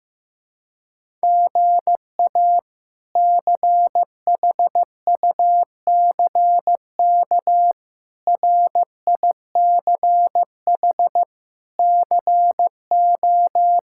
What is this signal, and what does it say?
Morse code, heard as a keyed tone, about 16 wpm, 700 Hz. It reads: GA CHUCK RICH CO
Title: K1USN SST QSO Practice - 15wpm